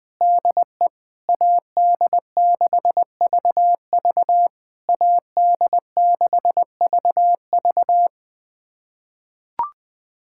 Morse code, heard as a keyed tone, about 20 wpm, 700 Hz. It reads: DE AD6VV AD6VV
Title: SOTA QSO Practice 20wpm